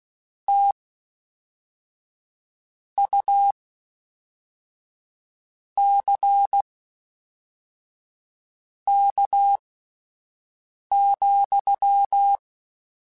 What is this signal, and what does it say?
Morse code, heard as a keyed tone, about 16 wpm, spaced as for 4 wpm, 800 Hz. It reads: TUCK,
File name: Misc / Misc-mp3s / WordPairs sp @5